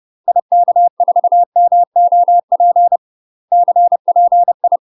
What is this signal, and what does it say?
Morse code, heard as a keyed tone, about 30 wpm, 700 Hz. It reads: IK4MOP CPI
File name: UK MASTER QSOs TRAINING / 30WPM / UK MASTER QSOs 0007